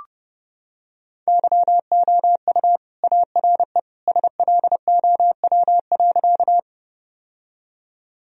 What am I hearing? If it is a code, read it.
Morse code, heard as a keyed tone, about 32 wpm, 700 Hz. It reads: YOU ARE SLOW.
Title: Sentences from Top 1000 Words - Review Part 2 - 30wpm